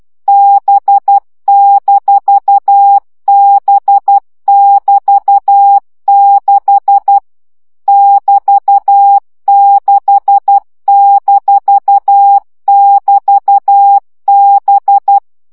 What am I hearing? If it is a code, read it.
Morse code, heard as a keyed tone, about 12 wpm, 800 Hz. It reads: B-B<BT>6 <BT>6-<BT>B
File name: CW gyakorlas b 6 egyenlo minusz 10 baud